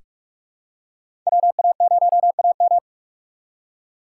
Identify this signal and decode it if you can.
Morse code, heard as a keyed tone, about 46 wpm, 700 Hz. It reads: WA0AM
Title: USA Call Signs - 5 Characters - 6x Character Spacing - 45wpm